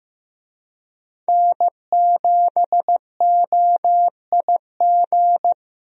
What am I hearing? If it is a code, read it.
Morse code, heard as a keyed tone, about 15 wpm, 700 Hz. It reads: N7OIG
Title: USA Call Signs - 5 Characters - Repeated 3x - 15wpm